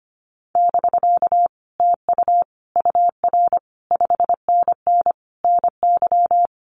Morse code, heard as a keyed tone, about 25 wpm, 700 Hz. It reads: <BK> TU UR 5NN NY
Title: Common QSO Phrases Encore 25wpm